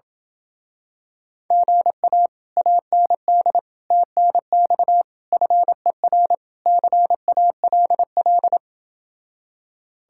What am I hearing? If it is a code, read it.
Morse code, heard as a keyed tone, about 28 wpm, 700 Hz. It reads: GA AND TNX FER CALL